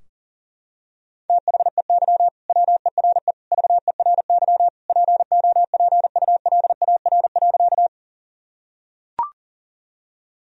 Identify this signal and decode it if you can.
Morse code, heard as a keyed tone, about 37 wpm, 700 Hz. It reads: THEY WERE VERY POPULAR.
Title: Sentences from Top 1000 Words - Review Part 3 - 40wpm